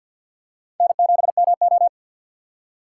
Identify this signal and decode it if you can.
Morse code, heard as a keyed tone, about 51 wpm, 700 Hz. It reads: N7MO
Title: USA Call Signs - 3x Character Spacing - 50wpm